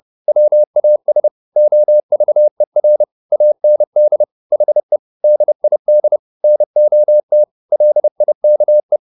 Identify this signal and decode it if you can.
Morse code, heard as a keyed tone, about 30 wpm, 600 Hz. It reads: WAS OVER AND HE DID NOT LIKE
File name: the old man and the sea / the old man and the sea 1 30wpm